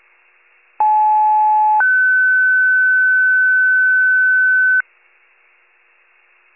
Two-tone page, heard as a one-second tone, then a three-second tone, about 860 then 1500 Hz. It is heard from 0.8 to 4.8 s.